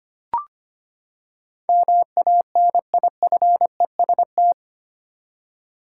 Morse code, heard as a keyed tone, about 25 wpm, 700 Hz. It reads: MANIFEST
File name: Eight Letter Words - 25wpm